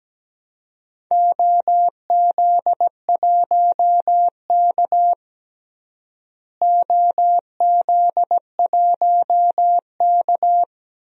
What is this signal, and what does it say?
Morse code, heard as a keyed tone, about 17 wpm, 700 Hz. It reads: OZ1K OZ1K
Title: DXCC-100 Call Signs - 4 Characters - Repeated 2x - 17wpm